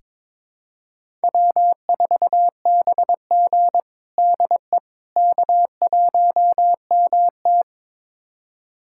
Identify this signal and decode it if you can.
Morse code, heard as a keyed tone, about 22 wpm, 700 Hz. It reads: W4BG DE K1MT
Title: USA Callsign DE Callsign - 4 Characters - 22wpm